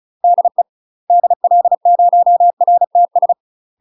Morse code, heard as a keyed tone, about 36 wpm, 700 Hz. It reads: DE DL0RTS
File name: UK MASTER QSOs TRAINING / 35WPM / UK MASTER QSOs 0038